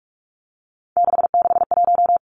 Morse code, heard as a keyed tone, about 46 wpm, 700 Hz. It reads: 661
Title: Three Challenging Letter-Number Combinations 45wpm